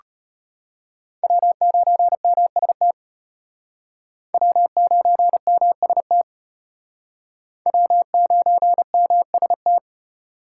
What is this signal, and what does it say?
Morse code, heard as a keyed tone, about 34 wpm, 700 Hz. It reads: W9MST W9MST W9MST